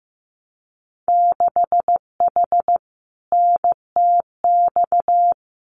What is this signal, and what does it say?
Morse code, heard as a keyed tone, about 15 wpm, 700 Hz. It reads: 6H NTX
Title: Winter Field Day Exchanges Encore 15wpm